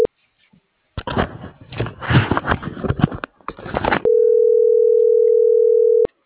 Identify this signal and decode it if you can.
Ringback tone (North America), 440 + 480 Hz.